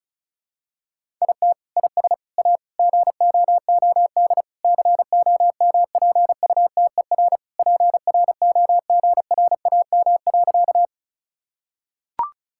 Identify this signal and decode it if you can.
Morse code, heard as a keyed tone, about 36 wpm, 700 Hz. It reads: IT IS A GOOD COMPUTER PROGRAM.
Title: Sentences from Top 1000 Words - Review Part 5 - 35wpm